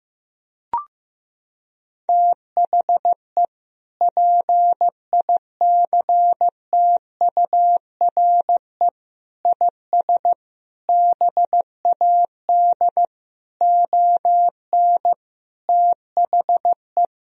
Morse code, heard as a keyed tone, about 16 wpm, 700 Hz. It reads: THE PICTURE IS BAD ON THE
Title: Sentences from Top 1000 Words - Review Part 4 - 15wpm